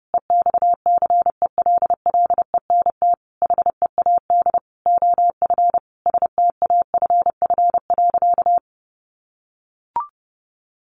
Morse code, heard as a keyed tone, about 30 wpm, 700 Hz. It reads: EXCELLENT HEAD OF STAFF.
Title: Sentences from Top 1200 Words 30wpm